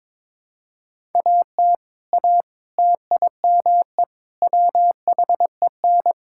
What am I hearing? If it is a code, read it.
Morse code, heard as a keyed tone, about 22 wpm, 700 Hz. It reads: AT A TIME WHEN
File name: practice / 22wpm / Sentences with 4-Word Phrases - 22wpm